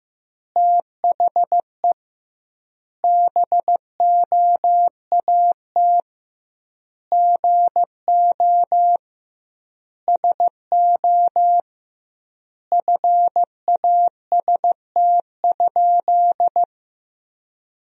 Morse code, heard as a keyed tone, about 15 wpm, 700 Hz. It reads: THE BOAT GO SO FAST?
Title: 2x Word-Spacing - Sentences from Top 500 Words - 15wpm